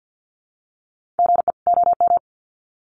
Morse code, heard as a keyed tone, about 50 wpm, 700 Hz. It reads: GE OM
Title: Common QSO Phrases 50wpm